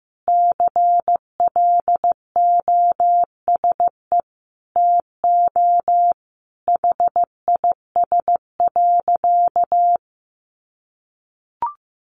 Morse code, heard as a keyed tone, about 15 wpm, 700 Hz. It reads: CLOSE TO HIS.
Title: Sentences from Top 900 Words 15wpm